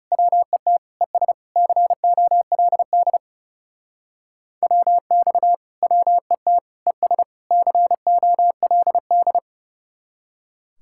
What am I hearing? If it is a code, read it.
Morse code, heard as a keyed tone, about 36 wpm, 700 Hz. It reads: WET ES COLD WX WET ES COLD
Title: Speed-Racing Common QSO Phrases - 30wpm